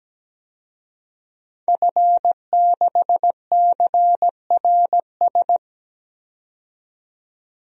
Morse code, heard as a keyed tone, about 17 wpm, 700 Hz. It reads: F6CRS
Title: DXCC-100 Call Signs - 5 Characters - 17wpm